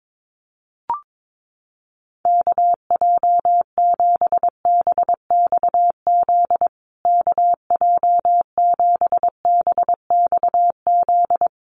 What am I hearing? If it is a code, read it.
Morse code, heard as a keyed tone, about 22 wpm, 700 Hz. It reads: KJ7BXZ KJ7BXZ
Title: USA Call Signs Repeated 3x 22wpm